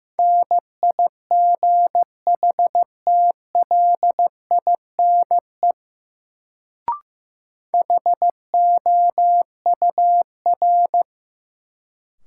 Morse code, heard as a keyed tone, about 15 wpm, 700 Hz. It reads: NIGHTLINE HOUR
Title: Sets of 5 Words - New Word Drill - 15wpm